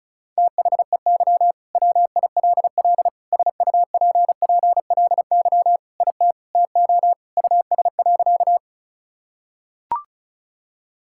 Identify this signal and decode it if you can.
Morse code, heard as a keyed tone, about 34 wpm, 700 Hz. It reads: THEY WILL SUPPLY IT TO US.
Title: Sentences from Top 1000 Words - Review Part 5 - 35wpm